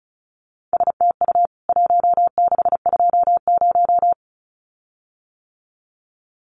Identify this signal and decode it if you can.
Morse code, heard as a keyed tone, about 34 wpm, 700 Hz. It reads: STU 1620
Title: CWT Exchange 35wpm